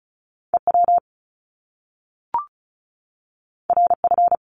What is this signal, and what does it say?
Morse code, heard as a keyed tone, about 36 wpm, 700 Hz. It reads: EW RF
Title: Top Two Letter Combinations in Words 35wpm